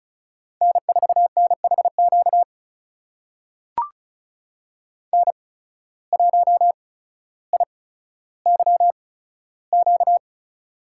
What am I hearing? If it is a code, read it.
Morse code, heard as a keyed tone, about 36 wpm, spaced as for 15 wpm, 700 Hz. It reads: N4NHQ N1IYQ
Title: USA Call Signs - 5 Characters - 8x Character Spacing - 35wpm